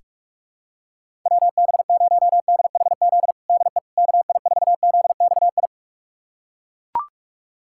Morse code, heard as a keyed tone, about 41 wpm, 700 Hz. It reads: WB0DHZ DE KI4ZXI
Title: USA Callsign DE Callsign - 6 Characters - 45wpm